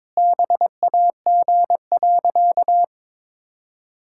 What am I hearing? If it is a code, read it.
Morse code, heard as a keyed tone, about 22 wpm, 700 Hz. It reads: BAG.
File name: practice / 22wpm / Sentences with 3-Word Phrases - 22wpm